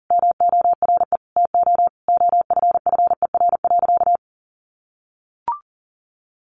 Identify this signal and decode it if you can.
Morse code, heard as a keyed tone, about 40 wpm, 700 Hz. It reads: MORE TO OFFER.
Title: Sentences from Top 1100 Words 40wpm